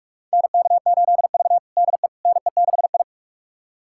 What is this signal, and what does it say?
Morse code, heard as a keyed tone, about 46 wpm, 700 Hz. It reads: NK8V DE NE6I
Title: USA Callsign DE Callsign - 4 Characters - 45wpm